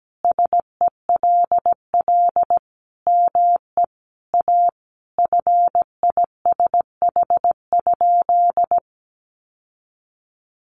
Morse code, heard as a keyed tone, about 17 wpm, 700 Hz. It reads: SELL ME A FISH?
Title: Sentences from Top 1100 Words 17wpm